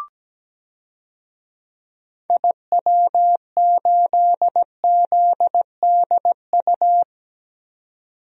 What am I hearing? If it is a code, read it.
Morse code, heard as a keyed tone, about 17 wpm, 700 Hz. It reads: IW8ZDU
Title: DXCC-100 Call Signs - 6 Characters - 17wpm